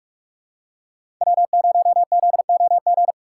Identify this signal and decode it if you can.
Morse code, heard as a keyed tone, about 42 wpm, 700 Hz. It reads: W0ZOG
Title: USA Call Signs - 5 Characters - 6x Character Spacing - 45wpm